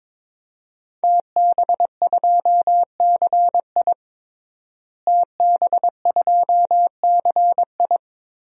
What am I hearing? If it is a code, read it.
Morse code, heard as a keyed tone, about 22 wpm, 700 Hz. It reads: TB2CI TB2CI
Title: DXCC-100 Call Signs - 5 Characters - Repeated 2x - 22wpm